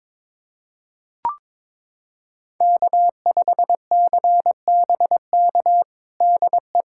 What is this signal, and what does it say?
Morse code, heard as a keyed tone, about 22 wpm, 700 Hz. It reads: K5CBK DE
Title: USA Callsign DE Callsign 22wpm